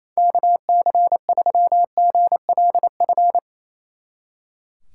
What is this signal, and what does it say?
Morse code, heard as a keyed tone, about 37 wpm, 700 Hz. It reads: KC3GLF